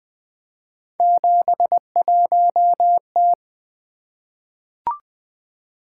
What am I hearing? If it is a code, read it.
Morse code, heard as a keyed tone, about 20 wpm, 700 Hz. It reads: 71T
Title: Three Letter-Number Combinations - 20wpm